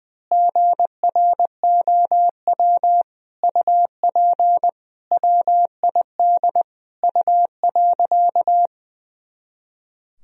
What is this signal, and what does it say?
Morse code, heard as a keyed tone, about 21 wpm, 700 Hz. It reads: GROW UP WID U.
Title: Sentences from Top 300 Words and Abbreviations 20wpm